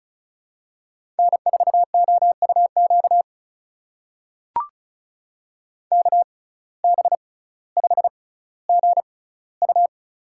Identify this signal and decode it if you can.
Morse code, heard as a keyed tone, about 36 wpm, spaced as for 18 wpm, 700 Hz. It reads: N4OUQ KB5GU
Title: USA Call Signs - 5 Characters - 6x Character Spacing - 35wpm